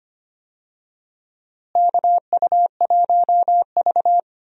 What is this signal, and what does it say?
Morse code, heard as a keyed tone, about 25 wpm, 700 Hz. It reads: KU1V
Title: USA Call Signs - 4 Characters - Repeated 3x - 25wpm